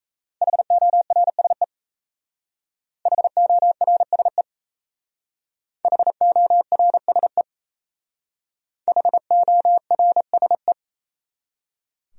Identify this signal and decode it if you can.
Morse code, heard as a keyed tone, about 35 wpm, 700 Hz. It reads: HORSE HORSE HORSE HORSE